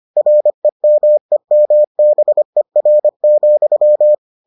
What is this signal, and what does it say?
Morse code, heard as a keyed tone, about 25 wpm, 600 Hz. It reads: REMEMBER,